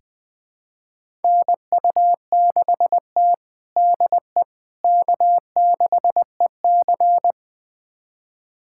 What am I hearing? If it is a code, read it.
Morse code, heard as a keyed tone, about 20 wpm, 700 Hz. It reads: NU6T DE K6EC
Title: USA Callsign DE Callsign - 4 Characters - 20wpm